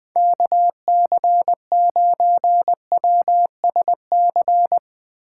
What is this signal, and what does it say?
Morse code, heard as a keyed tone, about 23 wpm, 700 Hz. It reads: KC9WSC
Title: Speed Racing - USA 2x3 Call Signs 20wpm